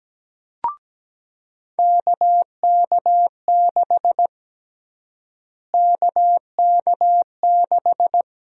Morse code, heard as a keyed tone, about 17 wpm, 700 Hz. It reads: KK6 KK6
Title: USA Call Sign Prefixes - Repeated 2x - 17wpm